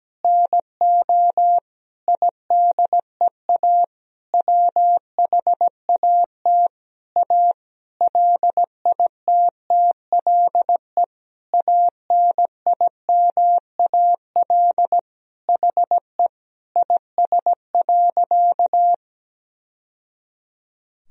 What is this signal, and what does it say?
Morse code, heard as a keyed tone, about 17 wpm, 700 Hz. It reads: NO IDEA WHAT A LITTLE ANIMAL HE IS.
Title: Sentences from Top 1000 Words - Review Part 1 - 17wpm